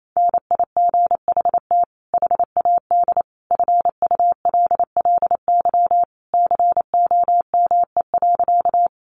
Morse code, heard as a keyed tone, about 28 wpm, 700 Hz. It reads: NIGHT HAD FULLY COME.
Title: Sentences from Top 1300 Words 28wpm